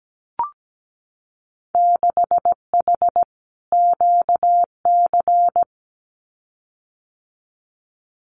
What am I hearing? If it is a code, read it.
Morse code, heard as a keyed tone, about 17 wpm, 700 Hz. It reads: 6H QC
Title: Winter Field Day Exchanges Encore 17wpm